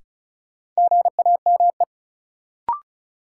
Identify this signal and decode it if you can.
Morse code, heard as a keyed tone, about 36 wpm, 700 Hz. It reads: GAME
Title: Sentences from Top 500 Words - New Word Drill - 35wpm